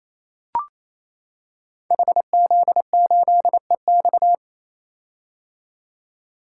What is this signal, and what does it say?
Morse code, heard as a keyed tone, about 27 wpm, 700 Hz. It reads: HZ8EX
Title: Code Groups - 5 Letter:Numbers - 28wpm